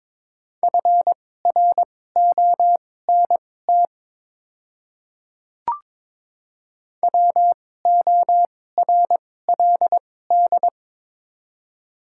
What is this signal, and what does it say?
Morse code, heard as a keyed tone, about 23 wpm, spaced as for 16 wpm, 700 Hz. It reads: FRONT WORLD
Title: 2x Character Spacing - Top 500 Words - 22wpm